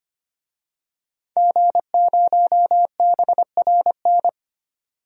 Morse code, heard as a keyed tone, about 25 wpm, 700 Hz. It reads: G0BRN
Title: DXCC-100 Call Signs - 5 Characters - 25wpm